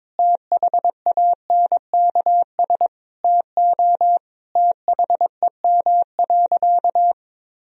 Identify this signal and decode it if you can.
Morse code, heard as a keyed tone, about 22 wpm, 700 Hz. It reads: THANKS TO THEM.
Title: Sentences from Top 1000 Words - Review Part 3 - 22wpm